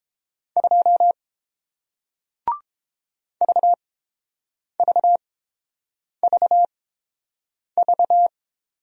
Morse code, heard as a keyed tone, about 30 wpm, spaced as for 7 wpm, 700 Hz. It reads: 2 VVVV